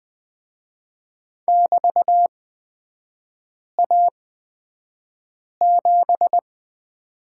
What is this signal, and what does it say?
Morse code, heard as a keyed tone, about 20 wpm, spaced as for 5 wpm, 700 Hz. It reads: <BT>A7
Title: 224 Lesson 36e - BT - Three Characters 20wpm